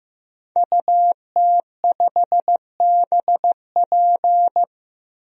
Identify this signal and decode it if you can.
Morse code, heard as a keyed tone, about 15 wpm, 700 Hz. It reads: UT5BP
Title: DXCC-100 Call Signs - 5 Characters - 15wpm